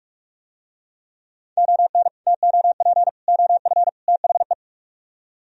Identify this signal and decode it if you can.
Morse code, heard as a keyed tone, about 41 wpm, 700 Hz. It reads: ON TOP OF THE